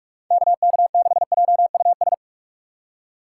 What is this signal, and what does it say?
Morse code, heard as a keyed tone, about 46 wpm, 700 Hz. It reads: KK6JUS
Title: USA Call Signs - 6x Character Spacing - 45wpm